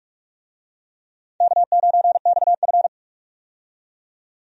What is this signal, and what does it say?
Morse code, heard as a keyed tone, about 43 wpm, 700 Hz. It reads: K9XF
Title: USA Call Signs - 4 Characters - 45wpm